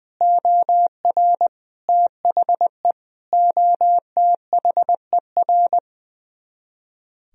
Morse code, heard as a keyed tone, about 20 wpm, 700 Hz. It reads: OR THE OTHER
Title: Speed-Racing - Sets of 5 Words 20wpm